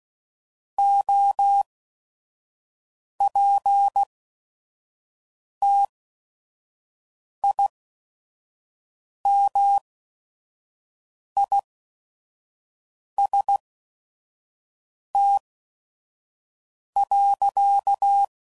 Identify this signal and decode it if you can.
Morse code, heard as a keyed tone, about 16 wpm, spaced as for 5 wpm, 800 Hz. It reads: OPTIMIST.
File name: Sayings / Quotes / Optimist